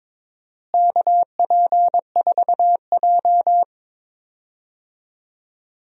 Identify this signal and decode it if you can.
Morse code, heard as a keyed tone, about 22 wpm, 700 Hz. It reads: KP4J
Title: DXCC-100 Call Signs - 4 Characters - Repeated 2x - 22wpm